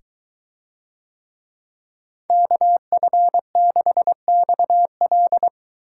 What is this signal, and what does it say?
Morse code, heard as a keyed tone, about 22 wpm, 700 Hz. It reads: KF6XL